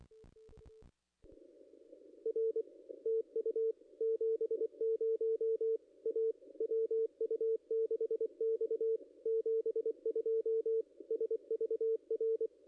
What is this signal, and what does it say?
Morse code, heard as a keyed tone, about 23 wpm, 450 Hz. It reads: R TU 70 A WU6X 72 SVR
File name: contests / ss-cw-k5tr-op-n5rz -0500